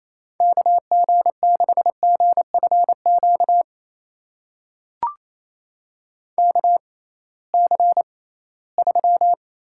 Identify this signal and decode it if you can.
Morse code, heard as a keyed tone, about 27 wpm, spaced as for 18 wpm, 700 Hz. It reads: KG6GFQ KC3
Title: USA Call Signs - 6 Characters - 6x Character Spacing - 28wpm